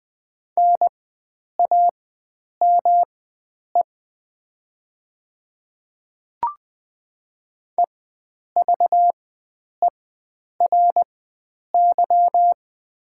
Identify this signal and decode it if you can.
Morse code, heard as a keyed tone, about 20 wpm, spaced as for 9 wpm, 700 Hz. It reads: NAME EVERY
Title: 4x Character Spacing - Top 300 Words - 20wpm